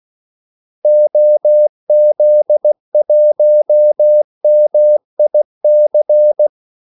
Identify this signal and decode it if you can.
Morse code, heard as a keyed tone, about 16 wpm, 600 Hz. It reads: OZ1MIC
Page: Kaldesignaler 16 wpm